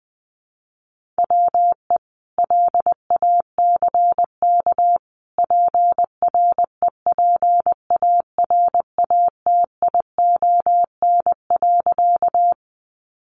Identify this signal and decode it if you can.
Morse code, heard as a keyed tone, about 20 wpm, 700 Hz. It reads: WE LACK PREPARATION.
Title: Sentences from Top 1400 Words 20wpm